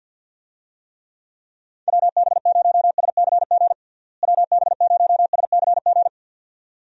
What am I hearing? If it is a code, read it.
Morse code, heard as a keyed tone, about 51 wpm, 700 Hz. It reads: WB0SCG WB0SCG
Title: USA Call Signs - 6 Characters - Repeated 2x - 50wpm